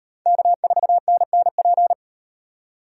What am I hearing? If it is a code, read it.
Morse code, heard as a keyed tone, about 36 wpm, 700 Hz. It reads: K4NNP